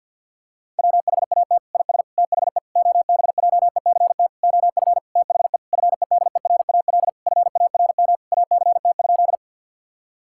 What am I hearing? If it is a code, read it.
Morse code, heard as a keyed tone, about 51 wpm, 700 Hz. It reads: WHAT IS THE OBJECT OF THE FEDERAL FARM ACT?
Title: Sentences from Top 1000 Words - Review Part 5 - 50wpm